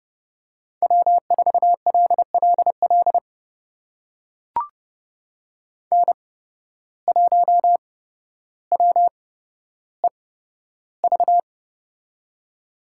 Morse code, heard as a keyed tone, about 32 wpm, spaced as for 13 wpm, 700 Hz. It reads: W4LLL N1WEV
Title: USA Call Signs - 5 Characters - 8x Character Spacing - 30wpm